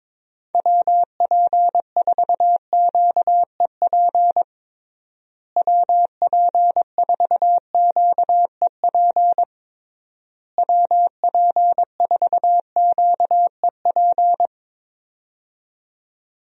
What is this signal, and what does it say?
Morse code, heard as a keyed tone, about 21 wpm, 700 Hz. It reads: WP4QEP WP4QEP WP4QEP
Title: USA Call Signs - 6 Characters - Repeated 3x - 22wpm